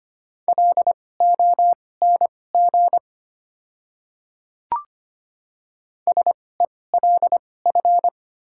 Morse code, heard as a keyed tone, about 25 wpm, spaced as for 18 wpm, 700 Hz. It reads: LONG SELF
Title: 2x Character Spacing - Top 200 Words - 25wpm